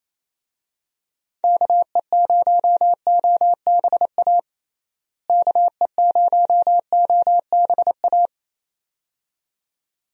Morse code, heard as a keyed tone, about 28 wpm, 700 Hz. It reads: KE0OBA KE0OBA
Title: USA Call Signs - 6 Characters - Repeated 2x - 28wpm